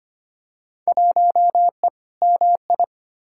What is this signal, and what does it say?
Morse code, heard as a keyed tone, about 25 wpm, 700 Hz. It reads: 1E MI
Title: Field Day Exchanges - 25wpm